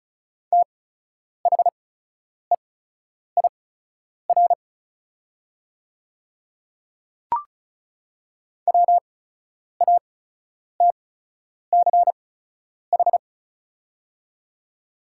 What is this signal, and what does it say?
Morse code, heard as a keyed tone, about 36 wpm, spaced as for 10 wpm, 700 Hz. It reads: THEIR WATCH
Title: 8x Character Spacing - Top 300 Words - 35wpm